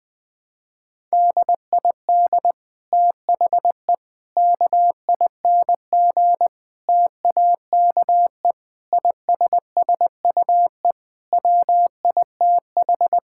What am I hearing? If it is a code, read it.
Morse code, heard as a keyed tone, about 20 wpm, 700 Hz. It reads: DID THE KING TAKE ISSUE WITH
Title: Sentences from Top 1000 Words - Review Part 6 - 20wpm